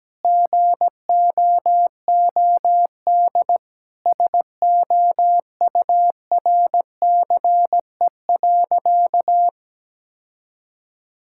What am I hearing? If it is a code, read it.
Morse code, heard as a keyed tone, about 17 wpm, 700 Hz. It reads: GOOD SOURCE.
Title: Sentences from Top 1000 Words - Review Part 5 - 17wpm